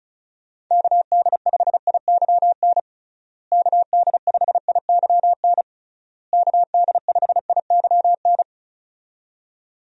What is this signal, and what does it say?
Morse code, heard as a keyed tone, about 36 wpm, 700 Hz. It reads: KD5IYN KD5IYN KD5IYN
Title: USA Call Signs - 6 Characters - Repeated 3x - 35wpm